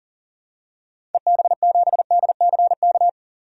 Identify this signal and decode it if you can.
Morse code, heard as a keyed tone, about 39 wpm, 700 Hz. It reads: EB7DCK
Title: DXCC-100 Call Signs - 6 Characters - 40wpm